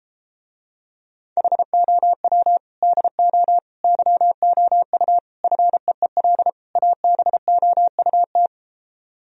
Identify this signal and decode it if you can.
Morse code, heard as a keyed tone, about 33 wpm, 700 Hz. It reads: HOW DO YOU FEEL ABOUT